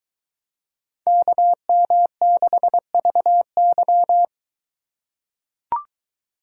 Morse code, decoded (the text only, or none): KM6VY